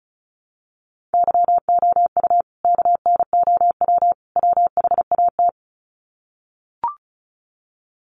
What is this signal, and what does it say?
Morse code, heard as a keyed tone, about 34 wpm, 700 Hz. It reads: YOU KNOW WHAT
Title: Sets of 3 Words 35wpm